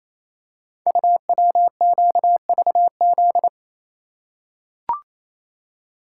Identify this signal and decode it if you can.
Morse code, heard as a keyed tone, about 28 wpm, 700 Hz. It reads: UWQVZ
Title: Code Groups - 5 Letters - 28wpm